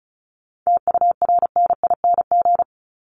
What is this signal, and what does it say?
Morse code, heard as a keyed tone, about 34 wpm, 700 Hz. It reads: TURNING
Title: ING Suffix - Top 100 Words with Base Words 35wpm